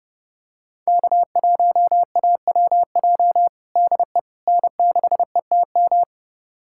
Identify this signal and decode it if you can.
Morse code, heard as a keyed tone, about 30 wpm, 700 Hz. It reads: K1AWJ DE N6ETM
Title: USA Callsign DE Callsign 30wpm